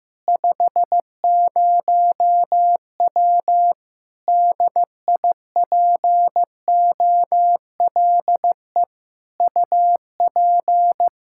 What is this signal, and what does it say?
Morse code, heard as a keyed tone, about 15 wpm, 700 Hz. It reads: 50W DIPOLE UP
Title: K1USN SST QSO Practice 15wpm